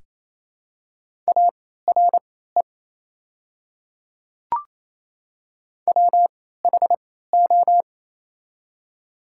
Morse code, heard as a keyed tone, about 28 wpm, spaced as for 16 wpm, 700 Hz. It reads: ARE WHO
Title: 3x Character Spacing - Top 200 Words - 28wpm